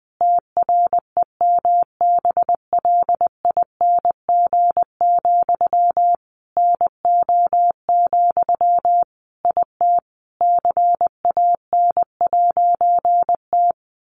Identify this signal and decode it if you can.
Morse code, heard as a keyed tone, about 19 wpm, 700 Hz. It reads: TREMBLING, NO, IT CAN'T
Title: A Sound Of Thunder 20wpm